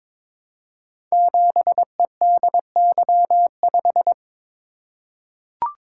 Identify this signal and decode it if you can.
Morse code, heard as a keyed tone, about 22 wpm, 700 Hz. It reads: 7EDY5
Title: Code Groups - 5 Letter:Numbers - 22wpm